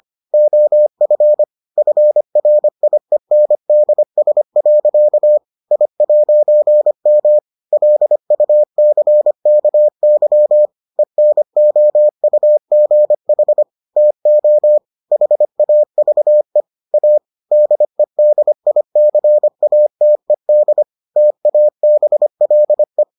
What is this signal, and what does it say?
Morse code, heard as a keyed tone, about 25 wpm, 600 Hz. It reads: OF FRIENDS. I'M LUCKY ENOUGH TO HAVE A DEDICATED TABLE